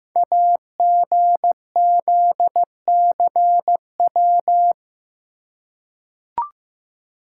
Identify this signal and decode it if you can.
Morse code, heard as a keyed tone, about 15 wpm, 700 Hz. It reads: AGZCW
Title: Code Groups - 5 Letters - 15wpm